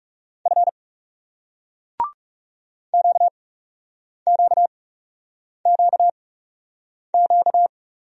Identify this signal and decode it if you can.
Morse code, heard as a keyed tone, about 37 wpm, spaced as for 8 wpm, 700 Hz. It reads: F QQQQ
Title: Speed-Racing Single Letters 30wpm